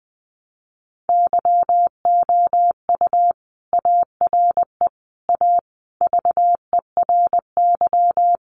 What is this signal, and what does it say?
Morse code, heard as a keyed tone, about 20 wpm, 700 Hz. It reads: YOU ARE A VERY
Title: Sentences from Top 1100 Words 20wpm